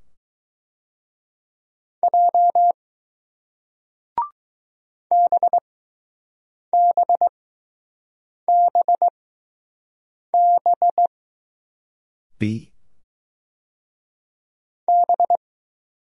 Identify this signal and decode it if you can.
Morse code, heard as a keyed tone, about 20 wpm, spaced as for 6 wpm, 700 Hz. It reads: J BBBB B